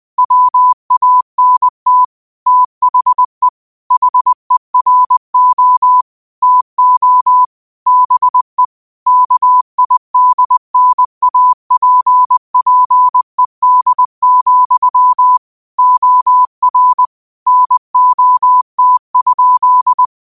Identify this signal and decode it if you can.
Morse code, heard as a keyed tone, about 20 wpm, 1000 Hz. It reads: WANT THE HERO TO BE KIDNAPPED, OR NOT?
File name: qotdfiles / 20wpm